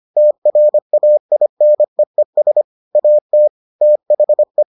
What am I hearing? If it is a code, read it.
Morse code, heard as a keyed tone, about 25 wpm, 600 Hz. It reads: TRAINEES AT THE